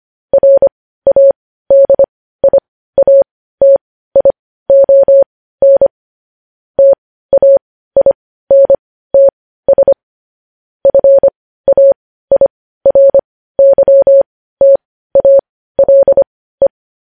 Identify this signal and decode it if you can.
Morse code, heard as a keyed tone, about 25 wpm, spaced as for 15 wpm, 550 Hz. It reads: RADIATION TAINTS FAIRYTALE
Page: news headlines in morse code at 15 wpm